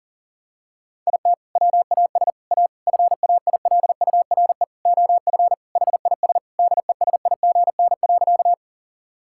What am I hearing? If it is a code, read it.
Morse code, heard as a keyed tone, about 37 wpm, 700 Hz. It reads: IT WAS A FAILURE OF HIS DESIGN.
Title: Sentences from Top 1000 Words - Review Part 6 - 40wpm